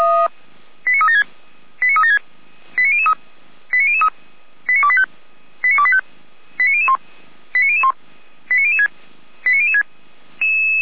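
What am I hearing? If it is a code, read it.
Five-tone selective call (ZVEI), heard as quick runs of five tones, tones of about 70 ms. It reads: 89376 89376 89003 89003 89285 89285 89001 89001 89006 89006
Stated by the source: ZVEI five-tone selective call received by SDR.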